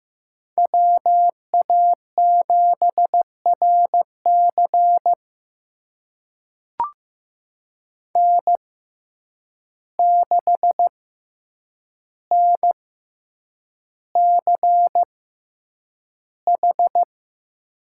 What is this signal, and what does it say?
Morse code, heard as a keyed tone, about 15 wpm, spaced as for 8 wpm, 700 Hz. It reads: WA7RC N6NCH
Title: USA Call Signs - 5 Characters - 6x Character Spacing - 15wpm